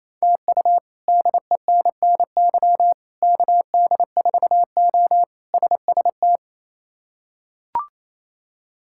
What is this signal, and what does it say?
Morse code, heard as a keyed tone, about 28 wpm, 700 Hz. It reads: TU DENNY KD4O SST
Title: K1USN SST QSO Practice - 28wpm